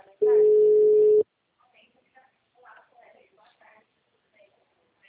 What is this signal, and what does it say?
Ringback tone (Europe, 425 Hz), with speech.